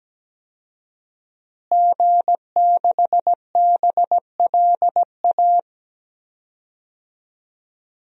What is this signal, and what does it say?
Morse code, heard as a keyed tone, about 17 wpm, 700 Hz. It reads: G6BLA
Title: DXCC-100 Call Signs - 5 Characters - Repeated 2x - 17wpm